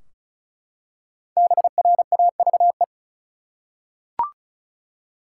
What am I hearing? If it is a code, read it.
Morse code, heard as a keyed tone, about 36 wpm, 700 Hz. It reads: BRAVE
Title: Five Letter Words - 35wpm